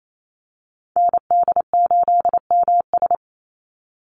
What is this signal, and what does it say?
Morse code, heard as a keyed tone, about 28 wpm, 700 Hz. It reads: ND8MS
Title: Call Signs Repeated 3x - 28wpm